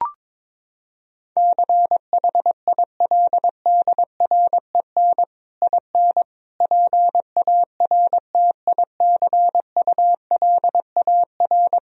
Morse code, heard as a keyed tone, about 22 wpm, 700 Hz. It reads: CHILDREN IN PARTICULAR
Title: Sentences from Top 1000 Words - Review Part 4 - 22wpm